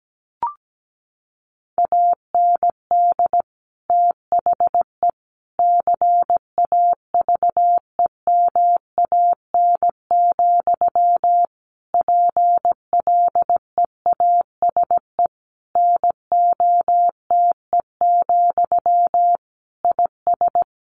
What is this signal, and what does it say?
Morse code, heard as a keyed tone, about 17 wpm, 700 Hz. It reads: AND THE CAVEMAN, PLEASE NOTE, IS
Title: A Sound Of Thunder 17wpm